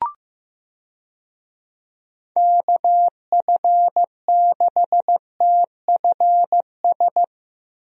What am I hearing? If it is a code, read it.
Morse code, heard as a keyed tone, about 16 wpm, 700 Hz. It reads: KF6TFS
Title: USA Call Signs - 6 Characters - Repeated 2x - 15wpm